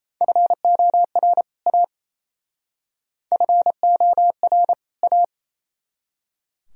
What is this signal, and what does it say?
Morse code, heard as a keyed tone, about 37 wpm, 700 Hz. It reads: FOR A FOR A